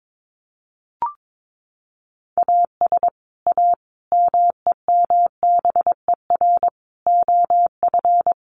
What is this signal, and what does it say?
Morse code, heard as a keyed tone, about 22 wpm, 700 Hz. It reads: AS A MEMBER OF
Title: Sets of 5 Words 22wpm